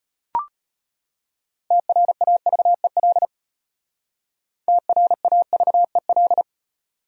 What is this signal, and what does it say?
Morse code, heard as a keyed tone, about 34 wpm, 700 Hz. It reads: TRAVEL TRAVEL